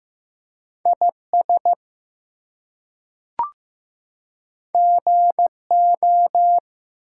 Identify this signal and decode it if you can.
Morse code, heard as a keyed tone, about 15 wpm, 700 Hz. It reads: IS GO
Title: Two Letter Words - 15wpm